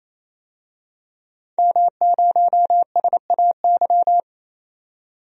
Morse code, heard as a keyed tone, about 27 wpm, 700 Hz. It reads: M0SAY
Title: DXCC-100 Call Signs - 5 Characters - 28wpm